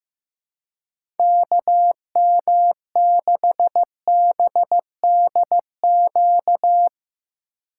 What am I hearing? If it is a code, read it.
Morse code, heard as a keyed tone, about 15 wpm, 700 Hz. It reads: KM6BDQ
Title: USA Call Signs - 6 Characters - 6x Character Spacing - 15wpm